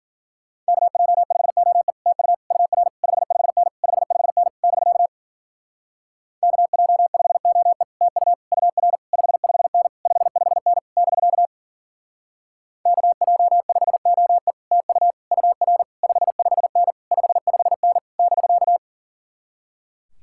Morse code, heard as a keyed tone, about 50 wpm, 700 Hz. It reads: KJ5OE TU UR 55N 55N <BK> KJ5OE TU UR 55N 55N <BK> KJ5OE TU UR 55N 55N <BK>
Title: Speed-Racing Common QSO Phrases Encore - 40wpm